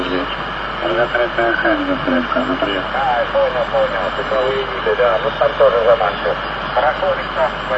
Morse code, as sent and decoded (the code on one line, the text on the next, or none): none